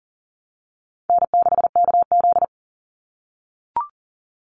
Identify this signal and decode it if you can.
Morse code, heard as a keyed tone, about 40 wpm, 700 Hz. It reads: N6KZ
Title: NA SOTA Chaser Callsigns 40wpm